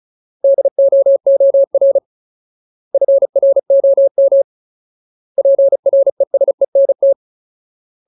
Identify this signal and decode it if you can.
Morse code, heard as a keyed tone, about 36 wpm, 550 Hz. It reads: DOOR FROM PRESENT
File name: Ultimate Word List / 35WPM Ultimate